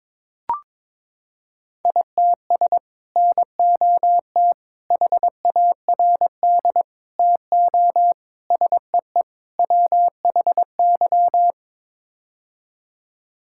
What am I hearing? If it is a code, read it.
Morse code, heard as a keyed tone, about 22 wpm, 700 Hz. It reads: ITS NOT HARD TO SEE WHY
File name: practice / 22wpm / Sets of 6 Words - 22wpm